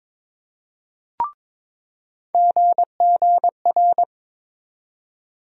Morse code, heard as a keyed tone, about 21 wpm, 700 Hz. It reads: GGR
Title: USA Call Sign Postfixes - 3 Letters 22wpm